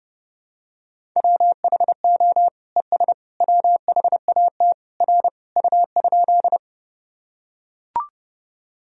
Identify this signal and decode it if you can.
Morse code, heard as a keyed tone, about 32 wpm, 700 Hz. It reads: WHO ES WHAT R U?
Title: Sentences from Top 100 Words and Abbreviations 30wpm